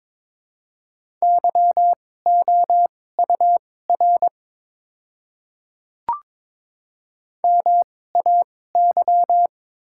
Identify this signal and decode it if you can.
Morse code, heard as a keyed tone, about 22 wpm, spaced as for 16 wpm, 700 Hz. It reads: YOUR MAY
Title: 2x Character Spacing - Top 200 Words - 22wpm